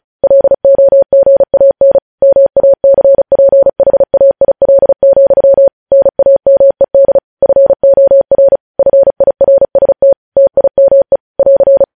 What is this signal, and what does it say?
Morse code, heard as a keyed tone, about 36 wpm, 550 Hz. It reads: LOGAN MACPHAIL, NAMED FOR FIRST TIME <AR>